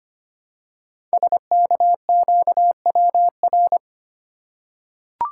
Morse code, heard as a keyed tone, about 25 wpm, 700 Hz. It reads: SKQWR
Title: Code Groups - 5 Letter:Numbers - 25wpm